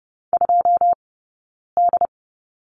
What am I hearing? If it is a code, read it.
Morse code, heard as a keyed tone, about 30 wpm, 700 Hz. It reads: 2 D
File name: Lesson 19d - Letter w - Two Characters 30wpm